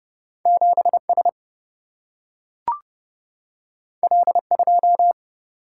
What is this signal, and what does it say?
Morse code, heard as a keyed tone, about 30 wpm, 700 Hz. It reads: 7 S L 2
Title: Single-Letter Single-Number 30wpm